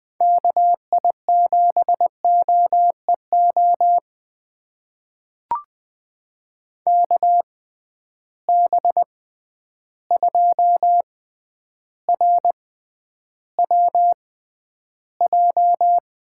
Morse code, heard as a keyed tone, about 20 wpm, spaced as for 10 wpm, 700 Hz. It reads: KI7OEO KB2RWJ
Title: USA Call Signs - 6 Characters - 6x Character Spacing - 20wpm